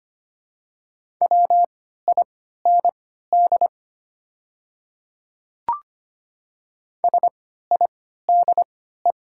Morse code, heard as a keyed tone, about 25 wpm, spaced as for 14 wpm, 700 Hz. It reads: WIND SIDE
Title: 3x Character Spacing - Top 500 Words - 25wpm